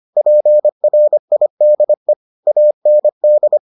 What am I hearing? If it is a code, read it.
Morse code, heard as a keyed tone, about 25 wpm, 600 Hz. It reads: PRIDE AND